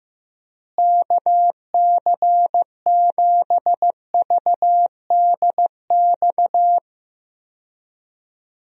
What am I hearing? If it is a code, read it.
Morse code, heard as a keyed tone, about 15 wpm, 700 Hz. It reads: KC7VDX
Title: USA Call Signs - 6 Characters - Repeated 3x - 15wpm